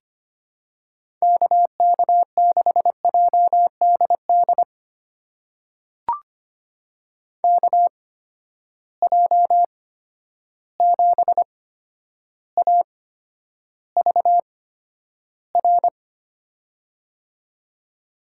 Morse code, heard as a keyed tone, about 25 wpm, spaced as for 11 wpm, 700 Hz. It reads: KK6JDD KJ7AVR
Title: USA Call Signs - 6 Characters - 8x Character Spacing - 25wpm